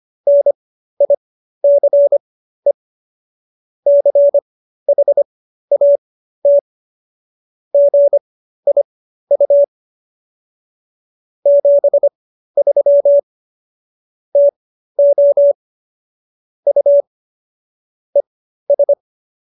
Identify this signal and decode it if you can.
Morse code, heard as a keyed tone, about 25 wpm, spaced as for 13 wpm, 570 Hz. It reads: NICE CHAT GIU 73 TO U ES